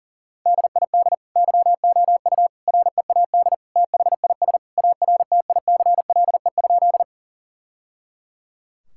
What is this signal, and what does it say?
Morse code, heard as a keyed tone, about 40 wpm, 700 Hz. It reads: DID YOU READ THIS ARTICLE?
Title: Sentences from Top 1000 Words - Review Part 5 - 40wpm